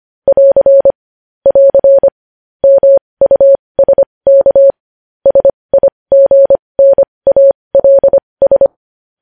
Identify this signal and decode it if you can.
Morse code, heard as a keyed tone, about 25 wpm, spaced as for 20 wpm, 550 Hz. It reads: <AR> <AR> MUSK SIGNALS
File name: podcast / News headlines-1x-25x20